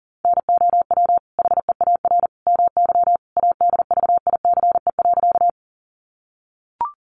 Morse code, heard as a keyed tone, about 40 wpm, 700 Hz. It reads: NOW HEAR MY ADVICE.
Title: Sentences from Top 1100 Words 40wpm